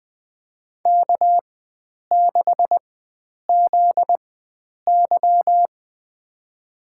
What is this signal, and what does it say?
Morse code, heard as a keyed tone, about 20 wpm, spaced as for 9 wpm, 700 Hz. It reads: K6ZY
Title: USA Call Signs - 4 Characters - 4x Character Spacing - 20wpm